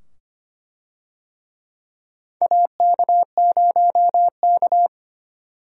Morse code, heard as a keyed tone, about 25 wpm, 700 Hz. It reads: AK0K
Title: USA Call Signs - 4 Characters - 25wpm